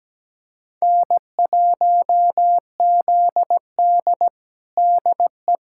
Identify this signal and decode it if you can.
Morse code, heard as a keyed tone, about 17 wpm, 700 Hz. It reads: N1ZD DE
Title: USA Callsign DE Callsign - 4 Characters - 17wpm